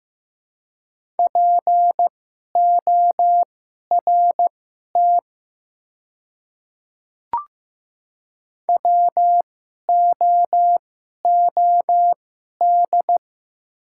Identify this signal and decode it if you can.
Morse code, heard as a keyed tone, about 15 wpm, spaced as for 11 wpm, 700 Hz. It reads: PORT WOOD
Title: 2x Character Spacing - Top 300 Words - 15wpm